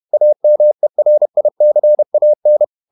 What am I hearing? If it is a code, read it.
Morse code, heard as a keyed tone, about 31 wpm, 600 Hz. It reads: AMERICAN
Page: Never 31wpm | CW med Gnister